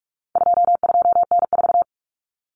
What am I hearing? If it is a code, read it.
Morse code, heard as a keyed tone, about 44 wpm, 700 Hz. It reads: 22N4
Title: Common Cut Numbers - 4-Digit 45wpm